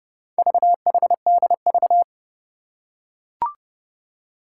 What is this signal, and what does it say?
Morse code, heard as a keyed tone, about 28 wpm, 700 Hz. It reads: VHDV
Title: Four Challenging Letter-Number Combinations 30wpm